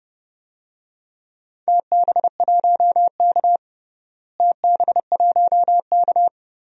Morse code, heard as a keyed tone, about 30 wpm, 700 Hz. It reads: TB1K TB1K
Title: DXCC-100 Call Signs - 4 Characters - Repeated 2x - 30wpm